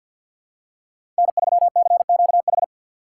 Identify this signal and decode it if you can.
Morse code, heard as a keyed tone, about 52 wpm, 700 Hz. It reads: N3CQH
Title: USA Call Signs - 3x Character Spacing - 50wpm